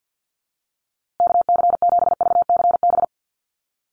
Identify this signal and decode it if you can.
Morse code, heard as a keyed tone, about 49 wpm, 700 Hz. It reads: KC7VCB
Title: USA Call Signs Repeated 2x 50wpm